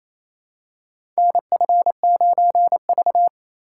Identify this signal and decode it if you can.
Morse code, heard as a keyed tone, about 28 wpm, 700 Hz. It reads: NF9V
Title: USA Call Signs - 4 Characters - 2x Character Spacing - 28wpm